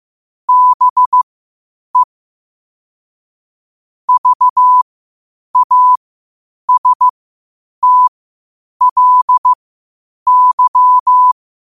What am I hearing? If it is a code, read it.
Morse code, heard as a keyed tone, about 15 wpm, spaced as for 8 wpm, 1000 Hz. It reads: BE VASTLY
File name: qotdfiles / 8wpm